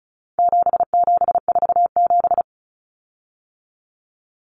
Morse code, heard as a keyed tone, about 34 wpm, 700 Hz. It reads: 7747
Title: Common Cut Numbers - 4-Digit 35wpm